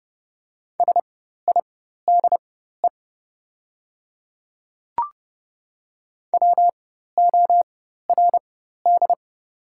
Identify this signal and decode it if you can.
Morse code, heard as a keyed tone, about 30 wpm, spaced as for 14 wpm, 700 Hz. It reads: SIDE WORD
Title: 4x Character Spacing - Top 100 Words - 30wpm